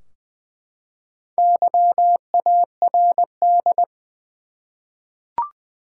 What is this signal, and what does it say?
Morse code, heard as a keyed tone, about 20 wpm, 700 Hz. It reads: YARD
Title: Four Letter Words - 20wpm